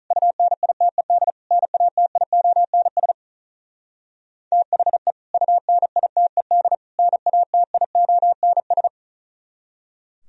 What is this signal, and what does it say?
Morse code, heard as a keyed tone, about 45 wpm, 700 Hz. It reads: UNITED NATIONS THE UNITED NATIONS